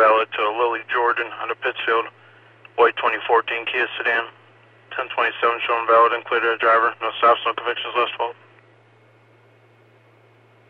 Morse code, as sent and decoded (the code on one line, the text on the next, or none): none